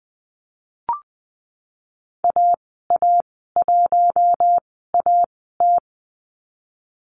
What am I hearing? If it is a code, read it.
Morse code, heard as a keyed tone, about 20 wpm, spaced as for 15 wpm, 700 Hz. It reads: AA1AT
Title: 088 Lesson 15g - Number 5 - Call Signs 20wpm